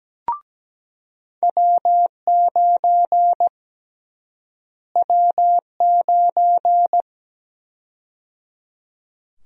Morse code, heard as a keyed tone, about 17 wpm, 700 Hz. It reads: W9 W9
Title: USA Call Sign Prefixes - Repeated 2x - 17wpm